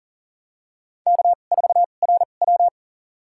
Morse code, heard as a keyed tone, about 43 wpm, spaced as for 30 wpm, 690 Hz. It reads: K4RW
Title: USA Call Signs - 4 Characters - 2x Character Spacing - 40wpm